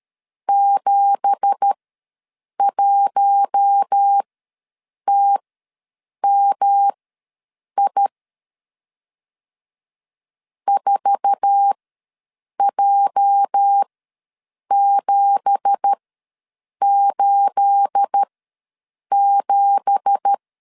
Morse code, heard as a keyed tone, about 13 wpm, spaced as for 7 wpm, 800 Hz. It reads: 71TMI 4J787
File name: LEZIONI / 9 2867 ws 12wpm 3431